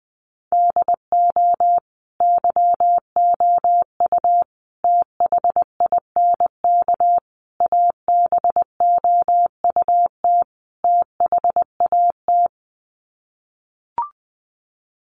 Morse code, heard as a keyed tone, about 20 wpm, 700 Hz. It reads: DO YOU THINK ABOUT THAT
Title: Sets of 5 Words Encore2 20wpm